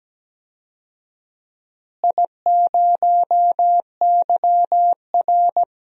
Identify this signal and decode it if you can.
Morse code, heard as a keyed tone, about 17 wpm, 700 Hz. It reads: I0YR
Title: DXCC-100 Call Signs - 4 Characters - Repeated 3x - 17wpm